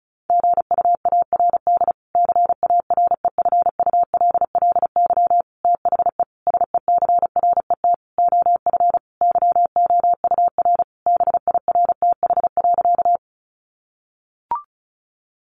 Morse code, heard as a keyed tone, about 34 wpm, 700 Hz. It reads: GUARD CAREFULLY THE SECRET OF YOUR BIRTH.
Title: Sentences from Top 1300 Words 35wpm